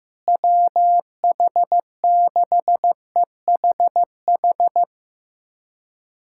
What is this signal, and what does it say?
Morse code, heard as a keyed tone, about 15 wpm, 700 Hz. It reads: WH6EHH
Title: USA Callsign DE Callsign - 6 Characters - 15wpm